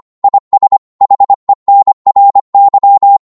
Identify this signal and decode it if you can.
Morse code, heard as a keyed tone, about 25 wpm, 800 Hz. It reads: IS HENRY
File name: Morse / qso-25wpm